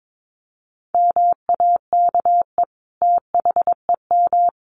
Morse code, heard as a keyed tone, about 22 wpm, 700 Hz. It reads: MAKE THEM
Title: Sentences from Top 900 Words 22wpm